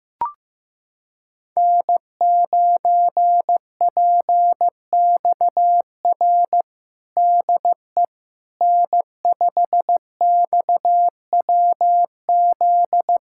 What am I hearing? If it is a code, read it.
Morse code, heard as a keyed tone, about 16 wpm, 700 Hz. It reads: N9PXR DE N5XWZ
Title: USA Callsign DE Callsign - 5 Characters - 15wpm